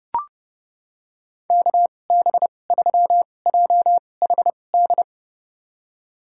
Morse code, heard as a keyed tone, about 30 wpm, spaced as for 22 wpm, 700 Hz. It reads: KB3JHD
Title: USA Call Signs - 6 Characters - 2x Character Spacing - 30wpm